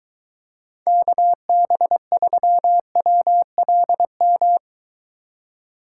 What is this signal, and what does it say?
Morse code, heard as a keyed tone, about 23 wpm, 700 Hz. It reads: KB3WLM